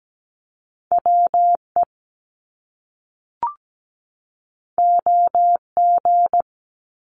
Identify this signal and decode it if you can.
Morse code, heard as a keyed tone, about 17 wpm, 700 Hz. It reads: WE OG
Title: Top Two Letter Combinations in Words 17wpm